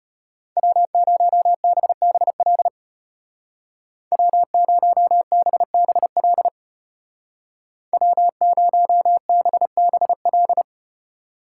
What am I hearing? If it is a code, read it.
Morse code, heard as a keyed tone, about 35 wpm, 700 Hz. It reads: W0BBL W0BBL W0BBL